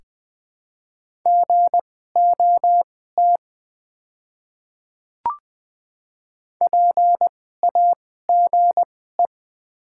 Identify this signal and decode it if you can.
Morse code, heard as a keyed tone, about 20 wpm, spaced as for 14 wpm, 700 Hz. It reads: GOT PAGE
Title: 2x Character Spacing - Top 300 Words - 20wpm